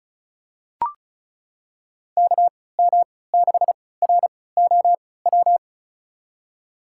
Morse code, heard as a keyed tone, about 36 wpm, spaced as for 20 wpm, 700 Hz. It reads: KM6ROW
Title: USA Call Signs - 3x Character Spacing - 35wpm